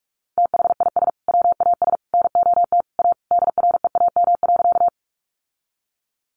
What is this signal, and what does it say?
Morse code, heard as a keyed tone, about 44 wpm, 700 Hz. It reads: THIS WAS NOT A DREAM.
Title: Sentences from Top 1500 Words 45wpm